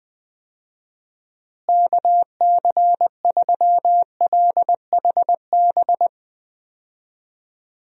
Morse code, heard as a keyed tone, about 20 wpm, 700 Hz. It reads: KC3LHB
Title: USA Call Signs - 6 Characters - Repeated 3x - 20wpm